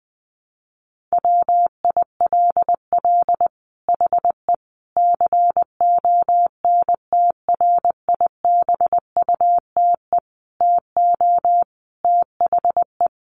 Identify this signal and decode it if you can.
Morse code, heard as a keyed tone, about 20 wpm, 700 Hz. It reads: WILL HE CONTRIBUTE TO THE
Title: Sentences from Top 1400 Words 20wpm